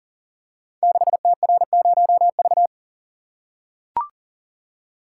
Morse code, heard as a keyed tone, about 39 wpm, 700 Hz. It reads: 6TR0V
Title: Code Groups - 5 Letter:Numbers - 40wpm